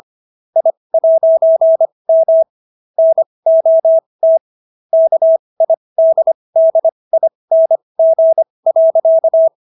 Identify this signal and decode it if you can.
Morse code, heard as a keyed tone, about 25 wpm, spaced as for 20 wpm, 650 Hz. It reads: I'M NOT KIDDING.